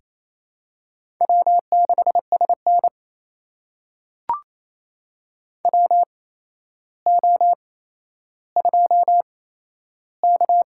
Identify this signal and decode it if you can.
Morse code, heard as a keyed tone, about 28 wpm, spaced as for 12 wpm, 700 Hz. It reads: W6SN WO2K
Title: USA Call Signs - 4 Characters - 8x Character Spacing - 28wpm